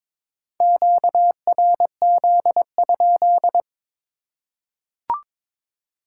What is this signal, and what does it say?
Morse code, heard as a keyed tone, about 22 wpm, 700 Hz. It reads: QRZ?
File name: Top 90 QSO Elements - 2x Character Spacing - 22wpm